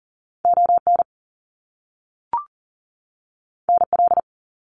Morse code, heard as a keyed tone, about 39 wpm, 700 Hz. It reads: ON NL
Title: Canadian Provinces and Territories 40wpm